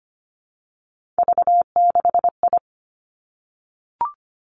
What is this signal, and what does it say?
Morse code, heard as a keyed tone, about 25 wpm, 700 Hz. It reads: V6I
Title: Three Challenging Letter-Number Combinations 25wpm